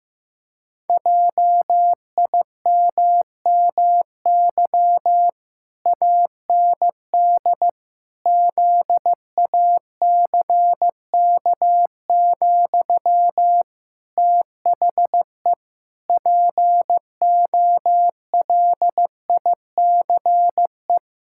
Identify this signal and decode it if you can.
Morse code, heard as a keyed tone, about 15 wpm, 700 Hz. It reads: JIMMY AND ZACK, THE POLICE
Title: Pangram Challenge v2 - 15wpm